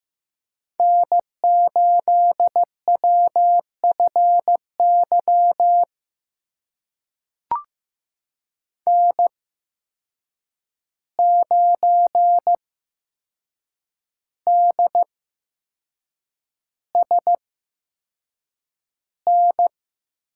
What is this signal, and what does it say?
Morse code, heard as a keyed tone, about 15 wpm, spaced as for 6 wpm, 700 Hz. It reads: N8WFY N9DSN
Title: USA Call Signs - 5 Characters - 8x Character Spacing - 15wpm